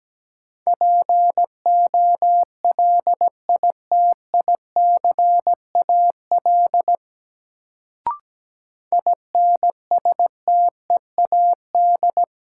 Morse code, heard as a keyed tone, about 17 wpm, 700 Hz. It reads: POLITICAL INSTEAD
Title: Sentences from Top 800 Words - New Word Drill - 17wpm